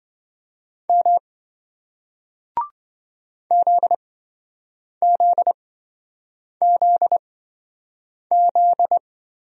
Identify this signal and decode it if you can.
Morse code, heard as a keyed tone, about 28 wpm, spaced as for 7 wpm, 700 Hz. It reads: M ZZZZ